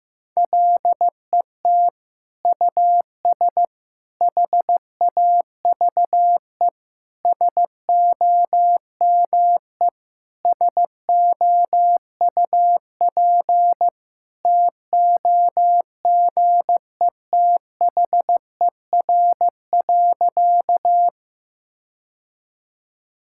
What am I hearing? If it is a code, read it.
Morse code, heard as a keyed tone, about 15 wpm, 700 Hz. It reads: LET US HAVE SOME SOUP TOGETHER.
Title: Sentences from Top 1000 Words - Review Part 6 - 15wpm